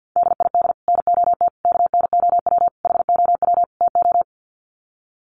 Morse code, heard as a keyed tone, about 49 wpm, 700 Hz. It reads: DID NOT KNOW HOW TO
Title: Sets of 5 Words 50wpm